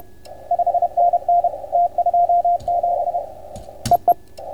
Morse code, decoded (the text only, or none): 5NN T27 EE